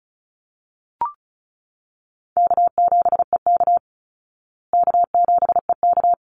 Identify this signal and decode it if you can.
Morse code, heard as a keyed tone, about 34 wpm, 700 Hz. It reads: K7EK K7EK
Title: Easy Call Signs Repeated 2x - 35wpm